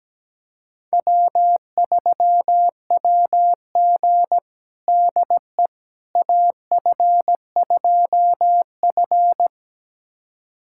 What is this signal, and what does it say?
Morse code, heard as a keyed tone, about 17 wpm, 700 Hz. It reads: W3WG DE AF2F
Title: USA Callsign DE Callsign - 4 Characters - 17wpm